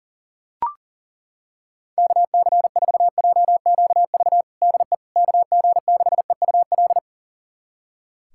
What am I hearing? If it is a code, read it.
Morse code, heard as a keyed tone, about 40 wpm, 700 Hz. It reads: KC4JQV DE KG6EUL
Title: USA Callsign DE Callsign - 6 Characters - 40wpm